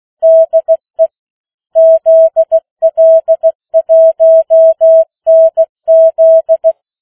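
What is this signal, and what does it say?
Morse code, heard as a keyed tone, about 16 wpm, 650 Hz. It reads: DE ZL1NZ
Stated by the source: Opening of the New Zealand national amateur radio net at 15 wpm, which meets on 3535.0 kHz.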